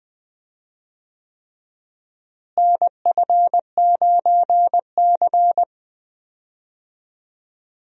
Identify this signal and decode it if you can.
Morse code, heard as a keyed tone, about 20 wpm, 700 Hz. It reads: NF9C
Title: USA Call Signs - 4 Characters - Repeated 3x - 20wpm